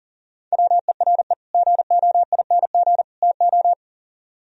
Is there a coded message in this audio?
Yes: Morse code from 0.5 to 3.7 s, heard as a keyed tone, about 51 wpm, 700 Hz. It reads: WERE GOING TO